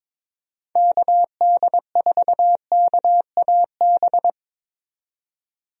Morse code, heard as a keyed tone, about 22 wpm, 700 Hz. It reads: KD4KAB
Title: USA Call Signs - 6 Characters - 22wpm